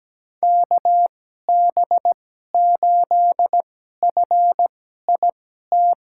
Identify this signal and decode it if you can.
Morse code, heard as a keyed tone, about 17 wpm, spaced as for 12 wpm, 700 Hz. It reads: KB8FIT
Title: USA Call Signs - 6 Characters - 2x Character Spacing - 17wpm